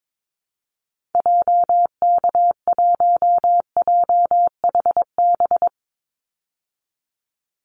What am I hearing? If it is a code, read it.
Morse code, heard as a keyed tone, about 22 wpm, 700 Hz. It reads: JK1JHB
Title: Call Signs Repeated 2x - 22wpm